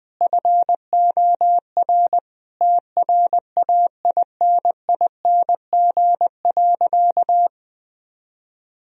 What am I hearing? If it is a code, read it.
Morse code, heard as a keyed tone, about 20 wpm, 700 Hz. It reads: FOR TRAINING.
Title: Sentences from Top 1000 Words - Review Part 4 - 20wpm